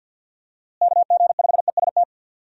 Morse code, heard as a keyed tone, about 50 wpm, 700 Hz. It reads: KG5EST
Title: USA Call Signs - 6 Characters - Repeated 3x - 50wpm